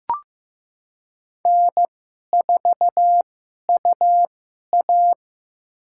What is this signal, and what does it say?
Morse code, heard as a keyed tone, about 15 wpm, spaced as for 11 wpm, 700 Hz. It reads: N4UA
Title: USA Call Signs - 2x Character Spacing - 15wpm